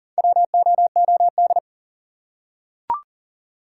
Morse code, heard as a keyed tone, about 40 wpm, 700 Hz. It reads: WOOD
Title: Four Letter Words - 40wpm